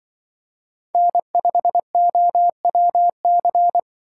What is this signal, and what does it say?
Morse code, heard as a keyed tone, about 24 wpm, 700 Hz. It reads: N5OWC
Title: Speed Racing - USA 1x3 Call Signs 20wpm